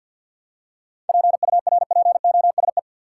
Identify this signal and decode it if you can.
Morse code, heard as a keyed tone, about 51 wpm, 700 Hz. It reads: PURPOSE
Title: Seven Letter Words - 50wpm